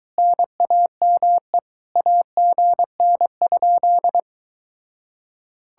Morse code, heard as a keyed tone, about 23 wpm, 700 Hz. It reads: NAME AGN?